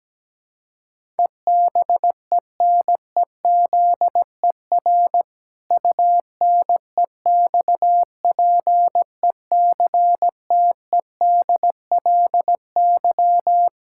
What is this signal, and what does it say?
Morse code, heard as a keyed tone, about 17 wpm, 700 Hz. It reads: EBENEZER UNEXPECTEDLY
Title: Pangram Challenge v2 - 17wpm